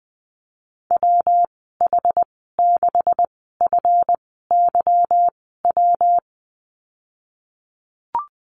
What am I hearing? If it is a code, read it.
Morse code, heard as a keyed tone, about 20 wpm, spaced as for 15 wpm, 700 Hz. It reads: WH6FYW
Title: 154 Lesson 25g - Letter p - Call Signs 20wpm